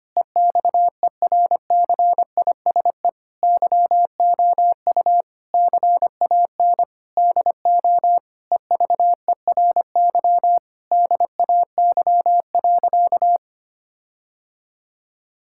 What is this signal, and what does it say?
Morse code, heard as a keyed tone, about 25 wpm, 700 Hz. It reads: EXERCISE YOU CAN DO EVERY DAY.
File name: practice / 25wpm / Difficult Sentences with Hints - 25wpm